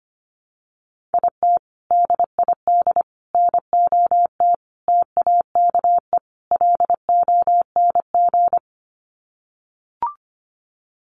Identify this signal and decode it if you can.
Morse code, heard as a keyed tone, about 25 wpm, 700 Hz. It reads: IT DID NOT TAKE LONG
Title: Sets of 5 Words Encore2 25wpm